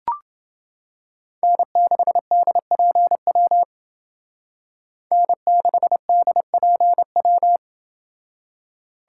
Morse code, heard as a keyed tone, about 29 wpm, 700 Hz. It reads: N6DPW N6DPW